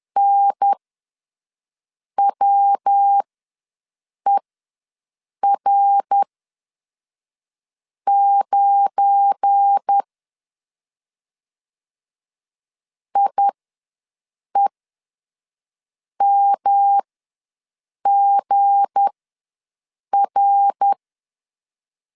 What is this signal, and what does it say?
Morse code, heard as a keyed tone, about 10 wpm, spaced as for 5 wpm, 800 Hz. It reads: NWER9 IEMGR
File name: LEZIONI / 7 GF9Z / 7 GF9 ws 10wpm 4034